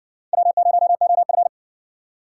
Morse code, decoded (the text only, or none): W0OF